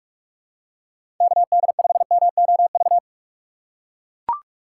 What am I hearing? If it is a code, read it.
Morse code, heard as a keyed tone, about 44 wpm, 700 Hz. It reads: KD5MOV
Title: USA Call Signs - 6 Characters - 6x Character Spacing - 45wpm